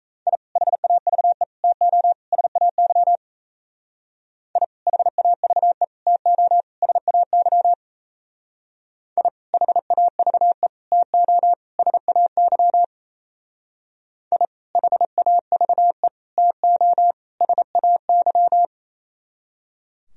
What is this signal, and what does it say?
Morse code, heard as a keyed tone, about 36 wpm, 700 Hz. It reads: I HAVE TO SAY I HAVE TO SAY I HAVE TO SAY I HAVE TO SAY